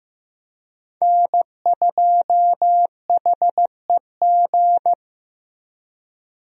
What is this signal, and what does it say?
Morse code, heard as a keyed tone, about 15 wpm, 700 Hz. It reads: N2HEG
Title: USA Call Signs - 5 Characters - Repeated 3x - 15wpm